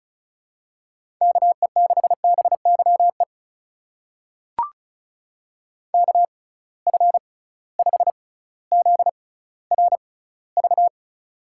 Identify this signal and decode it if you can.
Morse code, heard as a keyed tone, about 34 wpm, spaced as for 18 wpm, 700 Hz. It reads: KE6BYE KF5ZRV
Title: USA Call Signs - 6 Characters - 6x Character Spacing - 35wpm